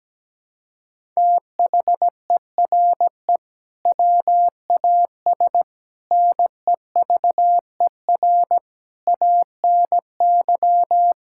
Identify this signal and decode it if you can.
Morse code, heard as a keyed tone, about 17 wpm, 700 Hz. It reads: THERE WAS NEVER ANY
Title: Sentences from Top 1000 Words - Review Part 6 - 17wpm